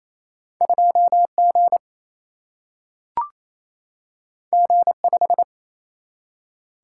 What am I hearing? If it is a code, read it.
Morse code, heard as a keyed tone, about 28 wpm, 700 Hz. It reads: 2G G5
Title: Single-Letter Single-Number 28wpm